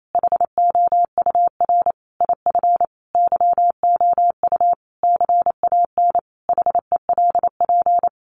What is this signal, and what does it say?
Morse code, heard as a keyed tone, about 28 wpm, 700 Hz. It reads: HOUR IF YOU CAN HELP
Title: Sentences from Top 1100 Words 28wpm